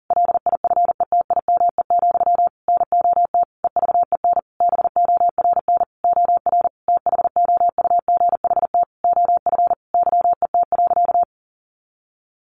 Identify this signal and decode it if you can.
Morse code, heard as a keyed tone, about 40 wpm, 700 Hz. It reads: LIFETIME, NOT EVEN BORN OR THOUGHT OF YET.
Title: A Sound Of Thunder 40wpm